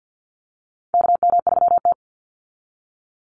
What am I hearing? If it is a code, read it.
Morse code, heard as a keyed tone, about 48 wpm, 700 Hz. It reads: KM3T
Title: Call Signs Repeated 2x - 50wpm